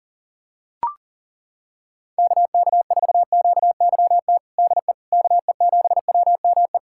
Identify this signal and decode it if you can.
Morse code, heard as a keyed tone, about 37 wpm, 700 Hz. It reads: KK4QYT DE KE7WME
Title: USA Callsign DE Callsign - 6 Characters - 40wpm